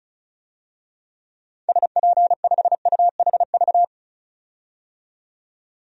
Morse code, heard as a keyed tone, about 36 wpm, 700 Hz. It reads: SP5UHV
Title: DXCC-100 Call Signs - 6 Characters - 35wpm